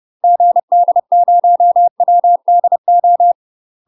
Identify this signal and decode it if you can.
Morse code, heard as a keyed tone, about 30 wpm, 700 Hz. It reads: GD0WDO
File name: UK MASTER QSOs TRAINING / 30WPM / UK MASTER QSOs 0039